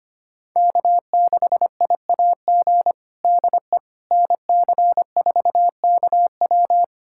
Morse code, heard as a keyed tone, about 25 wpm, 700 Hz. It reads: K6IAG DE NC4KW
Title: USA Callsign DE Callsign - 5 Characters - 25wpm